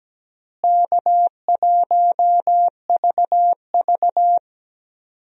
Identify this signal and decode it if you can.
Morse code, heard as a keyed tone, about 17 wpm, 700 Hz. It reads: K1VV
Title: USA Call Signs - 4 Characters - Repeated 3x - 17wpm